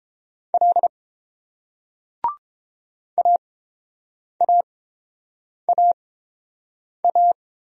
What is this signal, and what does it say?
Morse code, heard as a keyed tone, about 30 wpm, spaced as for 7 wpm, 700 Hz. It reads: L AAAA